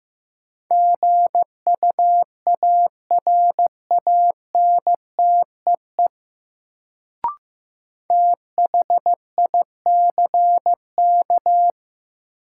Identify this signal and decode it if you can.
Morse code, heard as a keyed tone, about 16 wpm, 700 Hz. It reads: GUARANTEE THICK
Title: Sentences from Top 1600 Words - New Word Drill - 15wpm